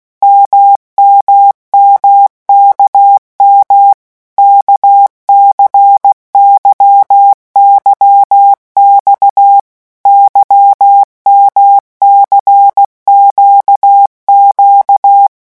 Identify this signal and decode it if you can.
Morse code, heard as a keyed tone, about 16 wpm, 800 Hz. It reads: MMMKM KCYYX YMCQQ
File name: morse mp3 / revision1-vitesse 15 mots